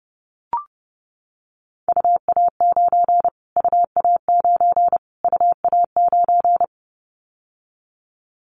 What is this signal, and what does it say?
Morse code, heard as a keyed tone, about 29 wpm, 700 Hz. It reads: UA9 UA9 UA9
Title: Callsign Prefixes for Top 25 DXCC - Repeated 3x - 30wpm